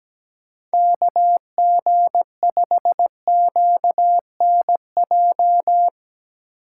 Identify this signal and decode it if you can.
Morse code, heard as a keyed tone, about 17 wpm, 700 Hz. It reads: KG5QNJ
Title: USA Call Signs - 4x Character Spacing - 17wpm